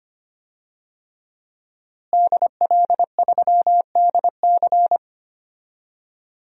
Morse code, heard as a keyed tone, about 25 wpm, 700 Hz. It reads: DL3DC
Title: DXCC-100 Call Signs - 5 Characters - Repeated 3x - 25wpm